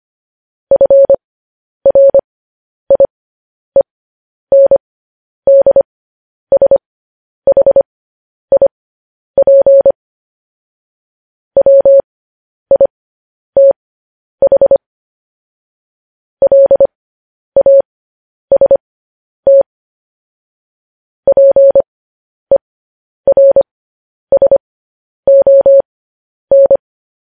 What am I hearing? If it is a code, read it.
Morse code, heard as a keyed tone, about 25 wpm, spaced as for 10 wpm, 550 Hz. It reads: FRIENDSHIP WITH LAST PERSON
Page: news-headlines-in-morse-code-at-10-wpm